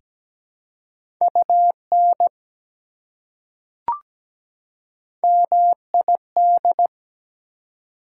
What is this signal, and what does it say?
Morse code, heard as a keyed tone, about 17 wpm, 700 Hz. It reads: UN MID
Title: Prefixes and Suffixes 17wpm